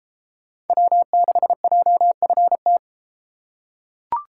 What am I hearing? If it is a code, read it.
Morse code, heard as a keyed tone, about 32 wpm, 700 Hz. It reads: W6JFT